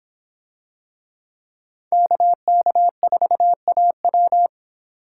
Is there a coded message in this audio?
Yes: Morse code from 1.9 to 4.5 s, heard as a keyed tone, about 25 wpm, 700 Hz. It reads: KK4AW